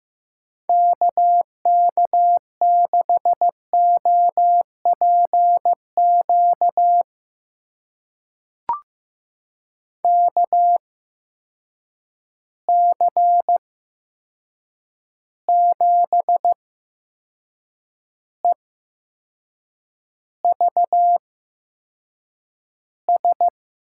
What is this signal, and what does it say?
Morse code, heard as a keyed tone, about 15 wpm, spaced as for 6 wpm, 700 Hz. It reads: KK6OPQ KC7EVS
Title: USA Call Signs - 6 Characters - 8x Character Spacing - 15wpm